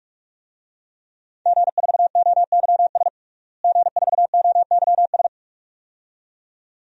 Morse code, heard as a keyed tone, about 46 wpm, 700 Hz. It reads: G4OYS G4OYS
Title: DXCC-100 Call Signs - 5 Characters - Repeated 2x - 45wpm